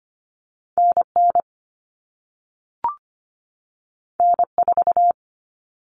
Morse code, heard as a keyed tone, about 25 wpm, 700 Hz. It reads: NN N4
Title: All Cut Numbers - 2-Digit 25wpm